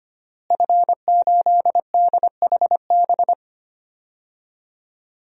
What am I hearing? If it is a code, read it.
Morse code, heard as a keyed tone, about 25 wpm, 700 Hz. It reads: F8DHB
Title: DXCC-100 Call Signs - 5 Characters - Repeated 3x - 25wpm